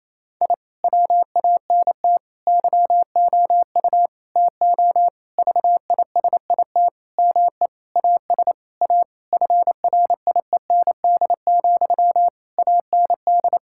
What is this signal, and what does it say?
Morse code, heard as a keyed tone, about 28 wpm, 700 Hz. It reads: I WANT YOU TO VISIT ME AS A FRIEND, AND
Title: Sentences from Top 1000 Words - Review Part 6 - 28wpm